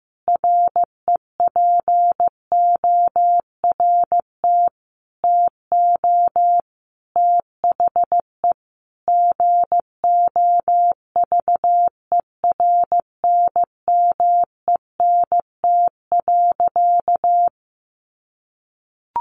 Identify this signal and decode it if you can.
Morse code, heard as a keyed tone, about 15 wpm, 700 Hz. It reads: REPORT TO THE GOVERNMENT.
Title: A Sound Of Thunder 15wpm